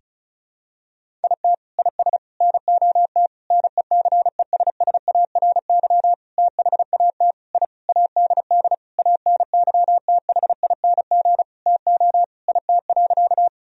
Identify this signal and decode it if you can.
Morse code, heard as a keyed tone, about 36 wpm, 700 Hz. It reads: IT IS NOT NECESSARY THAT I ADD ANYTHING TO IT.
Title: Sentences from Top 1000 Words - Review Part 1 - 35wpm